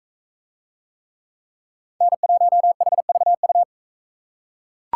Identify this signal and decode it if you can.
Morse code, heard as a keyed tone, about 39 wpm, 700 Hz. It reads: N1HVU